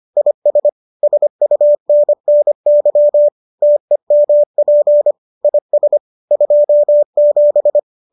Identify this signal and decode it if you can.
Morse code, heard as a keyed tone, about 25 wpm, 590 Hz. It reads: IS SUNNY TEMP IS 27